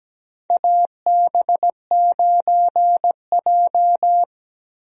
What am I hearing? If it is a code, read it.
Morse code, heard as a keyed tone, about 17 wpm, 700 Hz. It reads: AB9J
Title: USA Call Signs - 4 Characters - Repeated 2x - 17wpm